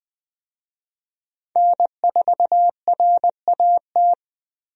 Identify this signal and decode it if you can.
Morse code, heard as a keyed tone, about 20 wpm, 700 Hz. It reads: N4RAT
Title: USA Call Signs - 5 Characters - Repeated 2x - 20wpm